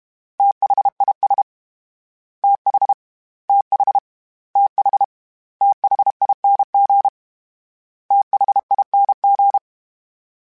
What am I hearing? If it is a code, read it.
Morse code, heard as a keyed tone, about 32 wpm, 800 Hz. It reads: THIS TH TH TH THING THING
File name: TH / TH mp3s / TH Prefixes 3x2 @30